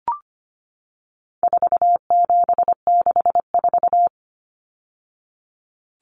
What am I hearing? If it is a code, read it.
Morse code, heard as a keyed tone, about 25 wpm, 700 Hz. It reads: 4764
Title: Numbers - 4-Digit 25wpm